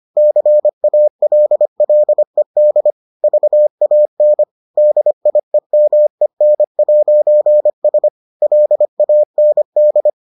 Morse code, heard as a keyed tone, about 25 wpm, 600 Hz. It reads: CALLED VAN DIEMEN'S LAND